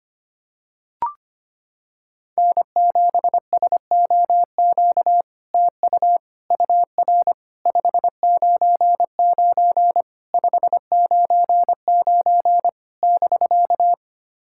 Morse code, heard as a keyed tone, about 25 wpm, 700 Hz. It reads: N7SOQ TU UR 599 599 <BK>
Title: SOTA QSO Practice 25wpm